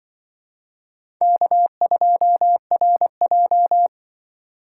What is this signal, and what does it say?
Morse code, heard as a keyed tone, about 24 wpm, 700 Hz. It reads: K2RJ
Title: Speed Racing - USA 1x2 Call Signs 20wpm